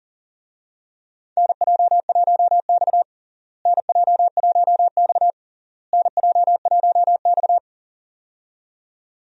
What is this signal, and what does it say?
Morse code, heard as a keyed tone, about 40 wpm, 700 Hz. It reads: NJ1X NJ1X NJ1X
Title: USA Call Signs - 4 Characters - Repeated 3x - 40wpm